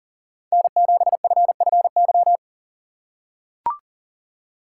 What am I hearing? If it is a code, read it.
Morse code, heard as a keyed tone, about 38 wpm, 700 Hz. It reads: N7FFY
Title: USA Call Signs - 5 Characters - 4x Character Spacing - 40wpm